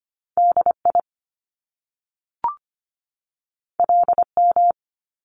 Morse code, heard as a keyed tone, about 25 wpm, 700 Hz. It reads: DI LM
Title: Top Two Letter Combinations in Words 25wpm